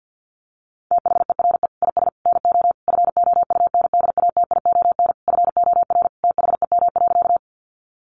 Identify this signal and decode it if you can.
Morse code, heard as a keyed tone, about 50 wpm, 700 Hz. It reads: THERE IS NO FOUNDATION FOR THEM.
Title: Sentences from Top 1200 Words 50wpm